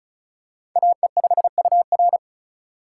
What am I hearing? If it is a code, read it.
Morse code, heard as a keyed tone, about 34 wpm, 700 Hz. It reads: AE5UR
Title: USA Call Signs - 5 Characters - 4x Character Spacing - 35wpm